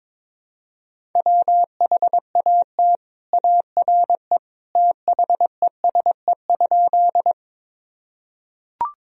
Morse code, heard as a keyed tone, about 22 wpm, 700 Hz. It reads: WHAT ARE THESE?
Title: Sentences from Top 1000 Words - Review Part 1 - 22wpm